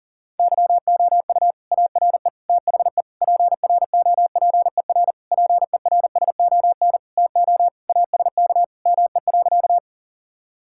Morse code, heard as a keyed tone, about 40 wpm, 700 Hz. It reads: YOU ARE THE PROPER PERSON TO ASK ME.
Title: Sentences from Top 1000 Words - Review Part 5 - 40wpm